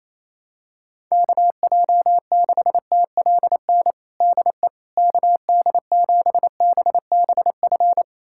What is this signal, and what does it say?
Morse code, heard as a keyed tone, about 28 wpm, 700 Hz. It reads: KJ6TLN DE KD7BBF
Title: USA Callsign DE Callsign - 6 Characters - 28wpm